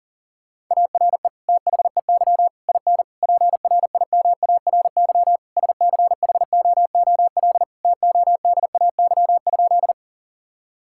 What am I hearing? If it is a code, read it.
Morse code, heard as a keyed tone, about 37 wpm, 700 Hz. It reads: ARE THEY IN PRIMARY SCHOOL TODAY?
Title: Sentences from Top 1000 Words - Review Part 5 - 40wpm